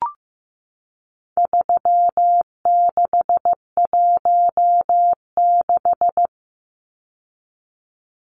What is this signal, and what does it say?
Morse code, heard as a keyed tone, about 15 wpm, 700 Hz. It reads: 3616
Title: Numbers - 4-Digit 15wpm